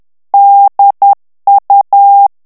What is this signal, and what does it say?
Morse code, heard as a keyed tone, about 10 wpm, 800 Hz. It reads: DU
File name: FRANCAIS TEXTE CLAIR 2 VITESSE 10 MOTS MINUTES